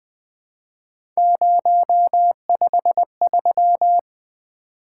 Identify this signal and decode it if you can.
Morse code, heard as a keyed tone, about 20 wpm, 700 Hz. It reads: 053
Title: Three Letter-Number Combinations - 20wpm